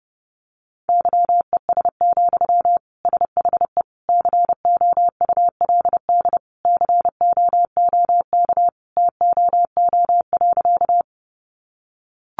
Morse code, heard as a keyed tone, about 30 wpm, 700 Hz. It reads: YES, SHE COULD COOK TOO.
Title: Sentences from Top 700 Words 30wpm